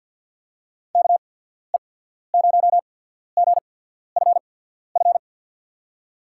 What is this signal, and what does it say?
Morse code, heard as a keyed tone, about 54 wpm, spaced as for 14 wpm, 700 Hz. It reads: KE0GFF
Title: USA Call Signs - 6 Characters - 8x Character Spacing - 50wpm